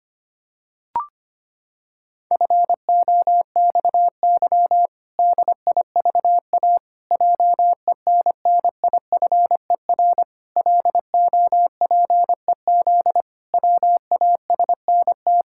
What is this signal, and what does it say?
Morse code, heard as a keyed tone, about 25 wpm, 700 Hz. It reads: FOXY DIVA JENNIFER LOPEZ WASNT
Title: Pangram Challenge v2 - 25wpm